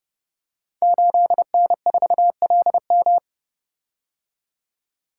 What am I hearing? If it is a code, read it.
Morse code, heard as a keyed tone, about 30 wpm, 700 Hz. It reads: 8N4LM
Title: Code Groups - 5 Letter:Numbers - 30wpm